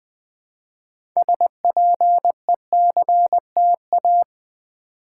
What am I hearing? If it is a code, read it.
Morse code, heard as a keyed tone, about 20 wpm, 700 Hz. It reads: SPECTA
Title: Build A Word - 3 Letters At A Time - 9-Letter Words - 20wpm